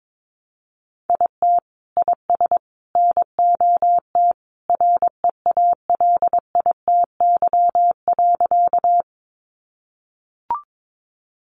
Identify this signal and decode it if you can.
Morse code, heard as a keyed tone, about 22 wpm, 700 Hz. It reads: IT IS NOT REALITY.
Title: Sentences from Top 1200 Words 22wpm